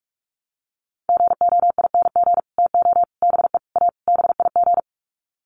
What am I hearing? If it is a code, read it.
Morse code, heard as a keyed tone, about 44 wpm, 690 Hz. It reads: GOING TO BE A BIG
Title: Sets of 5 Words Encore2 45wpm